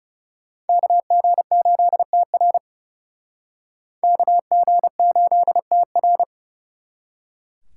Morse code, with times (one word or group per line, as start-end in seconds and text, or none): 0.7-2.6 KG8TR
4.0-6.2 KG8TR